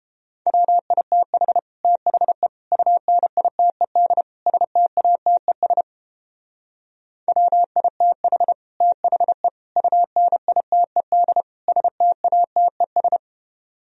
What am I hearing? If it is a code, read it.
Morse code, heard as a keyed tone, about 29 wpm, 700 Hz. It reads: WITH THE UNITED STATES WITH THE UNITED STATES